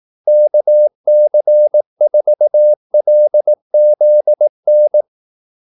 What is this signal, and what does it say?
Morse code, heard as a keyed tone, about 18 wpm, 600 Hz. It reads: KC4LZN
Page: Kaldesignaler 18 wpm